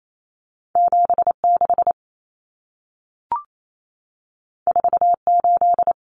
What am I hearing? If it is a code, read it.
Morse code, heard as a keyed tone, about 28 wpm, 700 Hz. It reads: 76 48
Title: Common Cut Numbers - 2-Digit 28wpm